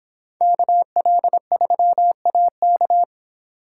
Morse code, heard as a keyed tone, about 25 wpm, 700 Hz. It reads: KL3AK